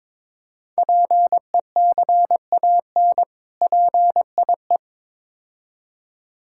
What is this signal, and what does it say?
Morse code, heard as a keyed tone, about 22 wpm, 700 Hz. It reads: PECAN PIE
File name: Difficult Words with Hints - 22wpm